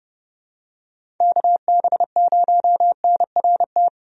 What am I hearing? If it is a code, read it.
Morse code, heard as a keyed tone, about 31 wpm, 700 Hz. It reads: KB0NRT
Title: USA Call Signs - 6 Characters - 30wpm